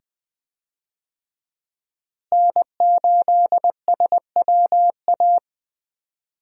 Morse code, heard as a keyed tone, about 21 wpm, 700 Hz. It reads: N8SWA
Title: USA Call Signs - 5 Characters - Repeated 2x - 20wpm